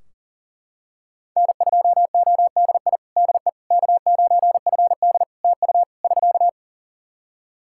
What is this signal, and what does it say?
Morse code, heard as a keyed tone, about 40 wpm, 700 Hz. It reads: N2ODI DE K9FD TU <SK>
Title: Practice QSO - Exchange 4 - Ending - 40wpm